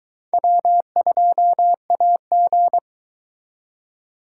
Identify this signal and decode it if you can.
Morse code, heard as a keyed tone, about 21 wpm, 700 Hz. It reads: W2AG